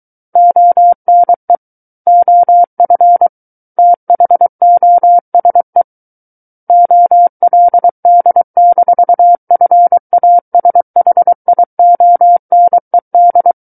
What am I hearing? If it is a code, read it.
Morse code, heard as a keyed tone, about 23 wpm, 700 Hz. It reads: ONE OF THOSE OLD-FASHIONED
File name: Book One / 14-Chapter 14 - 23 WPM